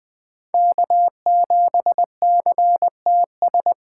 Morse code, heard as a keyed tone, about 20 wpm, 700 Hz. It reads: K7CTS
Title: USA Call Signs - 5 Characters - 2x Character Spacing - 20wpm